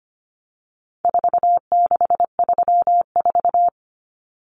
Morse code, heard as a keyed tone, about 25 wpm, 700 Hz. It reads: 4634
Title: Numbers - 4-Digit 25wpm